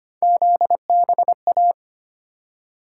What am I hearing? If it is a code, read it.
Morse code, heard as a keyed tone, about 25 wpm, 700 Hz. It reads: ZBA
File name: Three Letter-Number Combinations - 25wpm